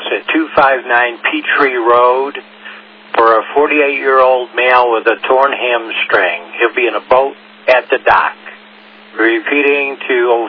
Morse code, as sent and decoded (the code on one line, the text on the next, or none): none